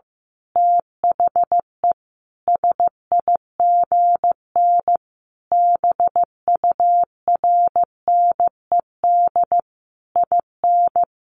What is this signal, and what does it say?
Morse code, heard as a keyed tone, about 15 wpm, 700 Hz. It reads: THE SIGN BURNED IN
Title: A Sound Of Thunder 15wpm